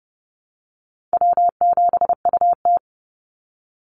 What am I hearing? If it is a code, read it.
Morse code, heard as a keyed tone, about 30 wpm, 700 Hz. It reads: W7UT
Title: Easy Call Signs Repeated 2x - 30wpm